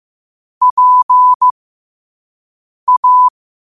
Morse code, heard as a keyed tone, about 15 wpm, spaced as for 5 wpm, 1000 Hz. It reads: PA
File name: qotdfiles / 5wpm